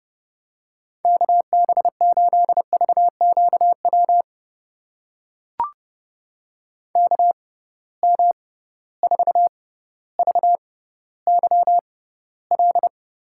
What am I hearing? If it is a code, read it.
Morse code, heard as a keyed tone, about 30 wpm, spaced as for 15 wpm, 700 Hz. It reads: KB8VQW KM4VYL
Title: USA Call Signs - 6 Characters - 6x Character Spacing - 30wpm